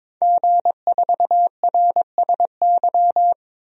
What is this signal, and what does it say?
Morse code, heard as a keyed tone, about 22 wpm, 700 Hz. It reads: G4RSY
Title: DXCC-100 Call Signs - 5 Characters - 22wpm